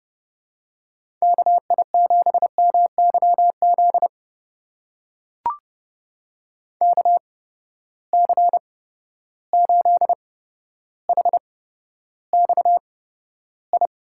Morse code, heard as a keyed tone, about 30 wpm, spaced as for 13 wpm, 700 Hz. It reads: KI7MYZ KC8HXI
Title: USA Call Signs - 6 Characters - 8x Character Spacing - 30wpm